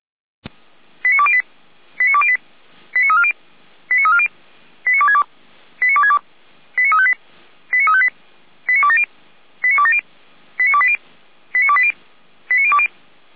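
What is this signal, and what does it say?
Five-tone selective call (ZVEI), heard as quick runs of five tones, tones of about 70 ms. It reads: 89208 89208 89340 89340 89262 89262 89358 89358 89270 89270 89280 89280 89020
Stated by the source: ZVEI five-tone selective call received by SDR.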